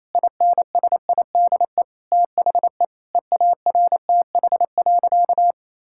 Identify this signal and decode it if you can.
Morse code, heard as a keyed tone, about 27 wpm, 700 Hz. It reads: INSIDE THE EARTH.
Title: Sentences from Top 1000 Words - Review Part 6 - 28wpm